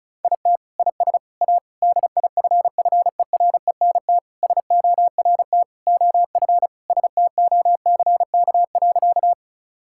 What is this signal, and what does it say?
Morse code, heard as a keyed tone, about 36 wpm, 700 Hz. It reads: IT IS A DIFFERENT SORT OF STOCK.
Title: Sentences from Top 1000 Words - Review Part 5 - 35wpm